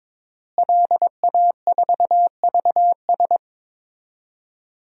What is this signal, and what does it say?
Morse code, heard as a keyed tone, about 22 wpm, 700 Hz. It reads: LA4VS
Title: DXCC-100 Call Signs - 5 Characters - Repeated 2x - 22wpm